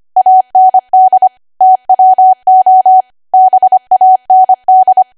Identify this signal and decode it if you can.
Morse code, heard as a keyed tone, about 25 wpm, 750 Hz. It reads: AND TWO BAND
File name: Archive / 25 WPM / 25WPM